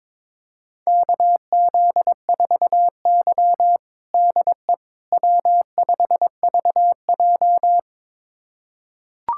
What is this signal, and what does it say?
Morse code, heard as a keyed tone, about 22 wpm, 700 Hz. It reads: KZ4Y DE W5VJ
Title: USA Callsign DE Callsign - 4 Characters - 22wpm